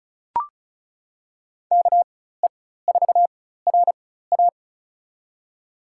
Morse code, heard as a keyed tone, about 36 wpm, spaced as for 16 wpm, 700 Hz. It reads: KE4RA
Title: USA Call Signs - 4x Character Spacing - 35wpm